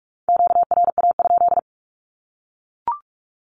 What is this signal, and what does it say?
Morse code, heard as a keyed tone, about 44 wpm, 700 Hz. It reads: QRA?
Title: Q-Codes Repeated 3x 45wpm